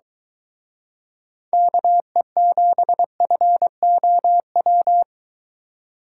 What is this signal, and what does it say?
Morse code, heard as a keyed tone, about 23 wpm, 700 Hz. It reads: KE7FOW